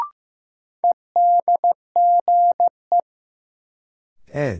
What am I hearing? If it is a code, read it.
Morse code, heard as a keyed tone, about 15 wpm, 700 Hz. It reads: EDGE
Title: Sentences from Top 1300 Words - New Word Drill - 15wpm